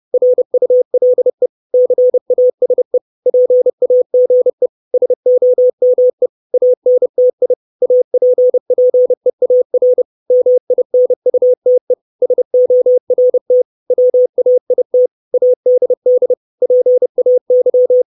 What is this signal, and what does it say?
Morse code, heard as a keyed tone, about 30 wpm, 500 Hz. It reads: RULE CASE PAGE SOME ANTI APPEAR MINUTE SORT WAIT ADD PAY
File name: 632 Words 30WPM